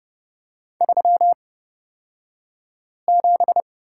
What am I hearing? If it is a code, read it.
Morse code, heard as a keyed tone, about 30 wpm, spaced as for 5 wpm, 700 Hz. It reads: 37
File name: course / 30:10wpm / 235 Lesson 38d - Letter q - Two Characters 30wpm